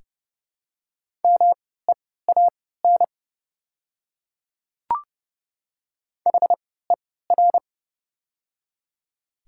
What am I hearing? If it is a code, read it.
Morse code, heard as a keyed tone, about 32 wpm, spaced as for 17 wpm, 700 Hz. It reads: MEAN HER
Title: 3x Character Spacing - Top 200 Words - 30wpm